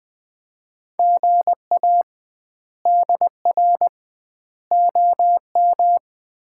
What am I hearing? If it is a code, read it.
Morse code, heard as a keyed tone, about 21 wpm, 700 Hz. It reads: GA DR OM
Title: Common QSO Phrases - 2x Word-Spacing - 20wpm